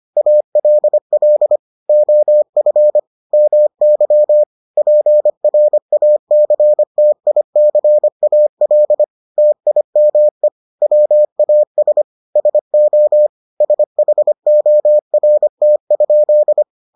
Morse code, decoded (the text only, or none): ALL OF MY PRACTICAL TIME WAS SO SHORT?